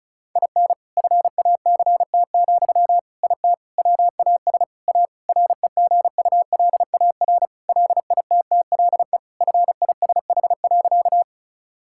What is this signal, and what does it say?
Morse code, heard as a keyed tone, about 36 wpm, 700 Hz. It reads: IN FACT, IT WAS A REGULAR LITTLE FISH.
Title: Sentences from Top 1000 Words - Review Part 6 - 35wpm